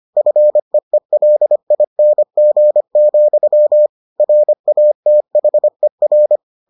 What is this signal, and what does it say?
Morse code, heard as a keyed tone, about 25 wpm, 600 Hz. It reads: FEELING, RATHER